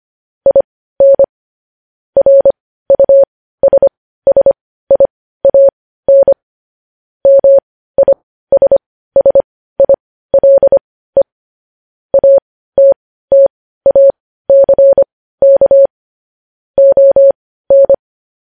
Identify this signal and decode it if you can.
Morse code, heard as a keyed tone, about 25 wpm, spaced as for 15 wpm, 550 Hz. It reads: IN RUSSIAN MISSILE ATTACK ON
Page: news-headlines-in-morse-code-at-15-wpm